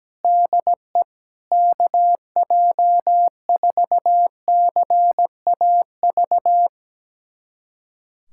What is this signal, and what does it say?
Morse code, heard as a keyed tone, about 17 wpm, 700 Hz. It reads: DE KJ4CAV
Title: USA Callsign DE Callsign 17wpm